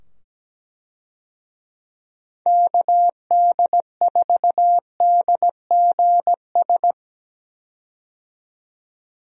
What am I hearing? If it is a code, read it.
Morse code, heard as a keyed tone, about 17 wpm, 700 Hz. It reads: KD4DGS
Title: USA Call Signs - 6 Characters - 17wpm